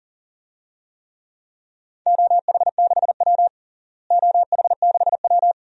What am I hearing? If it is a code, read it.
Morse code, heard as a keyed tone, about 40 wpm, 690 Hz. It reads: OH6W OH6W
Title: DXCC-100 Call Signs - 4 Characters - Repeated 2x - 40wpm